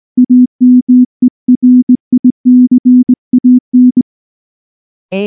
Morse code, heard as a keyed tone, about 18 wpm, 250 Hz. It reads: AMERICAN
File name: morse / morse-training-20WPM-5